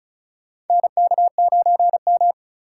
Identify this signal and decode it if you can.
Morse code, heard as a keyed tone, about 36 wpm, 700 Hz. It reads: NK9M
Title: USA Call Signs - 4 Characters - 3x Character Spacing - 35wpm